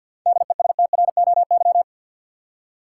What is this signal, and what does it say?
Morse code, heard as a keyed tone, about 50 wpm, 700 Hz. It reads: DESTROY
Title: Seven Letter Words - 50wpm